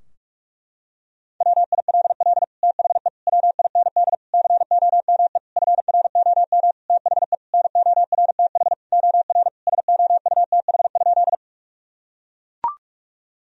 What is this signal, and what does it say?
Morse code, heard as a keyed tone, about 41 wpm, 700 Hz. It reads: WILL THE WIND COME FROM THE NORTH OR SOUTH?
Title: Sentences from Top 1000 Words - Review Part 2 - 45wpm